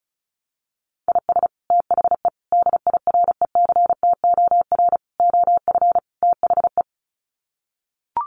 Morse code, heard as a keyed tone, about 34 wpm, 700 Hz. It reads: IS THE DIRECTOR OF THE
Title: Sets of 5 Words Encore2 35wpm